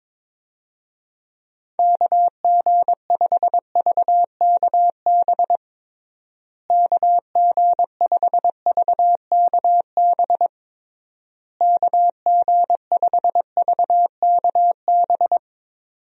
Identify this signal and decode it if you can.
Morse code, heard as a keyed tone, about 21 wpm, 700 Hz. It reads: KG5VKB KG5VKB KG5VKB
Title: USA Call Signs - 6 Characters - Repeated 3x - 22wpm